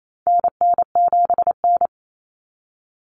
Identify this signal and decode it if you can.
Morse code, heard as a keyed tone, about 28 wpm, 700 Hz. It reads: NN7N
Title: Common Cut Numbers - 4-Digit 28wpm